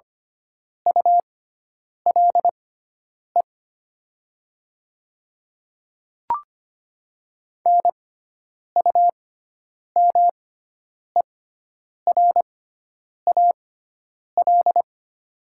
Morse code, heard as a keyed tone, about 25 wpm, spaced as for 9 wpm, 700 Hz. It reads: ULE NUMERAL
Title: 6x Character Spacing - Top 500 Words - 25wpm